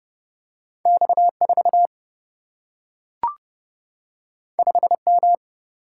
Morse code, heard as a keyed tone, about 30 wpm, 700 Hz. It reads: X4 5M
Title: Single-Letter Single-Number 30wpm